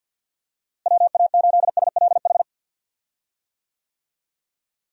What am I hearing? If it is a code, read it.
Morse code, heard as a keyed tone, about 53 wpm, 700 Hz. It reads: WA8SLH
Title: USA Call Signs - 6 Characters - Repeated 3x - 50wpm